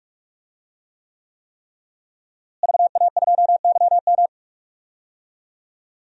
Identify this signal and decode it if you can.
Morse code, heard as a keyed tone, about 46 wpm, 700 Hz. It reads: VA2YM
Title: DXCC-100 Call Signs - 5 Characters - Repeated 3x - 45wpm